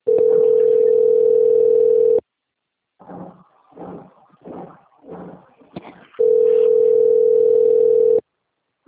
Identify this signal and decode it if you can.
Ringback tone (North America), 440 + 480 Hz.